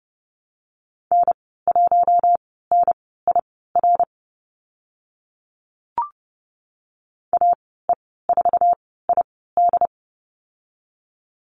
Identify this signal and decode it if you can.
Morse code, heard as a keyed tone, about 30 wpm, spaced as for 17 wpm, 700 Hz. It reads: N1NIR AE4ID
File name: Lesson 14g - Number 2 - Call Signs 30wpm